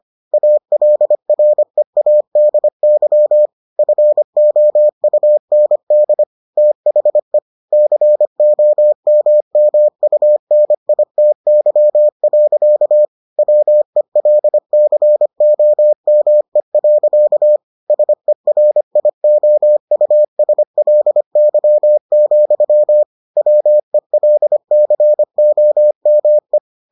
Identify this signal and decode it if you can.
Morse code, heard as a keyed tone, about 25 wpm, 600 Hz. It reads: ALREADY FOUND THE COMMUNITY. WELCOME. SERIOUSLY, WELCOME